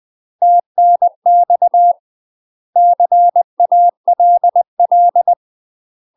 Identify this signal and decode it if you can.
Morse code, heard as a keyed tone, about 20 wpm, 700 Hz. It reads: TNX CALL
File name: UK MASTER QSOs TRAINING / 20WPM / UK MASTER QSOs 0029